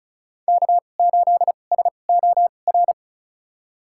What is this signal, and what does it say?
Morse code, heard as a keyed tone, about 34 wpm, spaced as for 25 wpm, 700 Hz. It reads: K8SOR
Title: USA Call Signs - 5 Characters - 2x Character Spacing - 35wpm